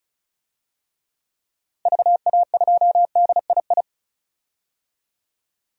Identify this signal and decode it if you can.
Morse code, heard as a keyed tone, about 36 wpm, 700 Hz. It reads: VA2DII
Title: DXCC-100 Call Signs - 6 Characters - Repeated 2x - 35wpm